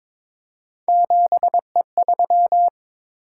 Morse code, heard as a keyed tone, about 22 wpm, 700 Hz. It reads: 7E3
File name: practice / 22wpm / Three Letter-Number Combinations - 22wpm